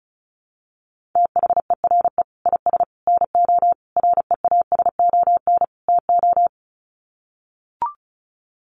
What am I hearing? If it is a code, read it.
Morse code, heard as a keyed tone, about 34 wpm, 700 Hz. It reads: THERE IS NO REASON TO
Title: Sets of 5 Words Encore 35wpm